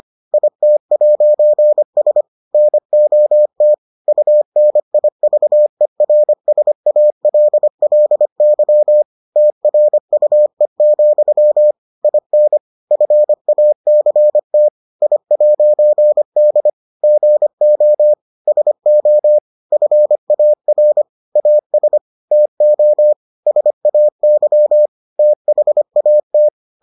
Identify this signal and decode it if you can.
Morse code, heard as a keyed tone, about 25 wpm, 600 Hz. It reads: IT'S NOT UNIVERSALLY TRUE, IN FACT I'D GO SO FAR AS TO SAY THAT